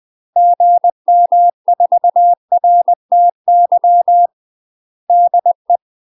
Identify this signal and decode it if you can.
Morse code, heard as a keyed tone, about 20 wpm, 700 Hz. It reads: GM4RTY DE
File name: UK MASTER QSOs TRAINING / 20WPM / UK MASTER QSOs 0043